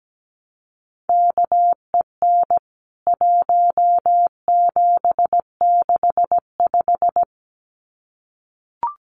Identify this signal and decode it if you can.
Morse code, heard as a keyed tone, about 17 wpm, 700 Hz. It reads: KEN 1765
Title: CWT Exchange 17wpm